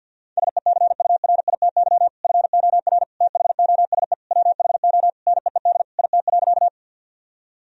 Morse code, heard as a keyed tone, about 50 wpm, 700 Hz. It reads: SECURITY FOR THOSE WHO NEED IT.
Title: Sentences from Top 1000 Words - Review Part 4 - 50wpm